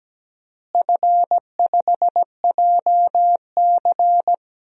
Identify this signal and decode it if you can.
Morse code, heard as a keyed tone, about 17 wpm, 700 Hz. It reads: F5JC
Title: DXCC-100 Call Signs - 4 Characters - Repeated 3x - 17wpm